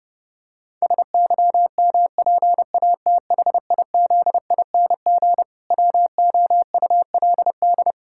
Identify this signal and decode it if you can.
Morse code, heard as a keyed tone, about 32 wpm, 700 Hz. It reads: SYMPATHIZING WOULD
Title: Pangram Challenge v2 - 30wpm